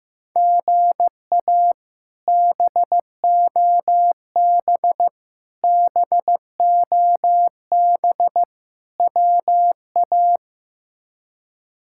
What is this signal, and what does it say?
Morse code, heard as a keyed tone, about 15 wpm, 700 Hz. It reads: GA BOB BOB WA
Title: K1USN SST QSO Practice - 15wpm